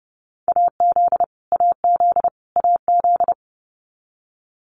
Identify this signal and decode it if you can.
Morse code, heard as a keyed tone, about 30 wpm, 700 Hz. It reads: AZ AZ AZ
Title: US States Repeated 3x 30wpm